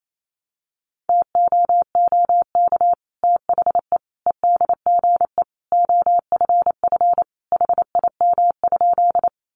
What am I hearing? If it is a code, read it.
Morse code, heard as a keyed tone, about 28 wpm, 700 Hz. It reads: TOOK THE EDGE OFF HIM?
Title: Sentences from Top 1300 Words 28wpm